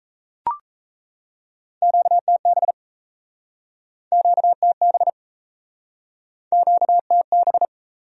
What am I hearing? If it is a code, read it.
Morse code, heard as a keyed tone, about 37 wpm, 700 Hz. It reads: QTB QTB QTB